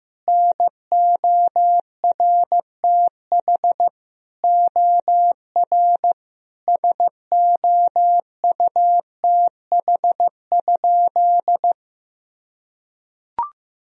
Morse code, heard as a keyed tone, about 15 wpm, 700 Hz. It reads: NORTH OR SOUTH?
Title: Sentences from Top 1000 Words - Review Part 2 - 15wpm